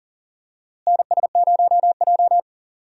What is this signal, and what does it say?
Morse code, heard as a keyed tone, about 40 wpm, 700 Hz. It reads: NS0J
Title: USA Call Signs - 4 Characters - Repeated 2x - 40wpm